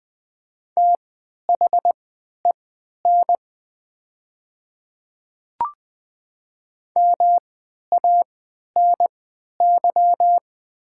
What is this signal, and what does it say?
Morse code, heard as a keyed tone, about 20 wpm, spaced as for 11 wpm, 700 Hz. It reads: THEN MANY
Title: 3x Character Spacing - Top 100 Words - 20wpm